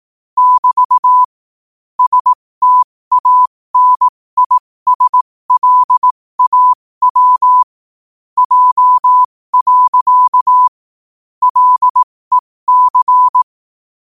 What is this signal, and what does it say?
Morse code, heard as a keyed tone, about 18 wpm, spaced as for 15 wpm, 1000 Hz. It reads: <BT> STANISLAW J. LEC
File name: qotdfiles / 15wpm